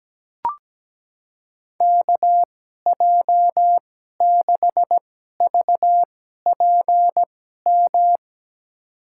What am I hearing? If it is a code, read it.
Morse code, heard as a keyed tone, about 17 wpm, spaced as for 12 wpm, 700 Hz. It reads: KJ6VPM
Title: USA Call Signs - 6 Characters - 2x Character Spacing - 17wpm